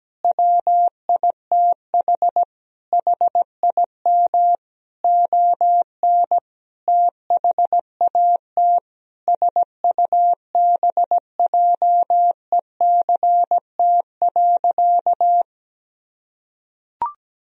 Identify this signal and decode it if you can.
Morse code, heard as a keyed tone, about 17 wpm, 700 Hz. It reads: WITH HIM ON THAT SUBJECT.
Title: Sentences from Top 1000 Words - Review Part 4 - 17wpm